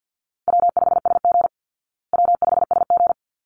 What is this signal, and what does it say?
Morse code, heard as a keyed tone, about 50 wpm, 700 Hz. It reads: W5SG W5SG
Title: Easy Call Signs Repeated 2x - 50wpm